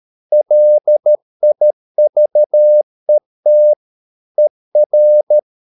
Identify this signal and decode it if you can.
Morse code, heard as a keyed tone, about 13 wpm, 600 Hz. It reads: LIVET ER
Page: Ferie 13 wpm